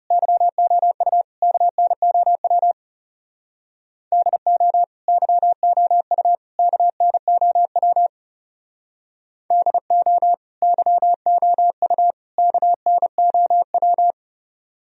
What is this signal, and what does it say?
Morse code, heard as a keyed tone, about 37 wpm, 700 Hz. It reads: YOU KNOW DO YOU KNOW DO YOU KNOW
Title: Speed-Racing - Sets of 3 Words 30wpm